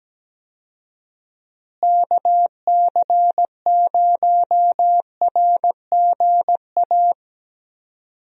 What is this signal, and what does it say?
Morse code, heard as a keyed tone, about 17 wpm, 700 Hz. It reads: KC0RGA
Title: USA Call Signs - 6 Characters - Repeated 2x - 17wpm